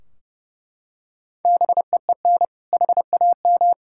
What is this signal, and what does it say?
Morse code, heard as a keyed tone, about 30 wpm, 700 Hz. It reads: BEEN HAM
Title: K1USN SST QSO Practice 30wpm